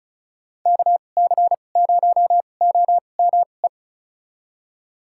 Morse code, heard as a keyed tone, about 34 wpm, spaced as for 25 wpm, 700 Hz. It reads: KC0OME
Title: USA Call Signs - 6 Characters - 2x Character Spacing - 35wpm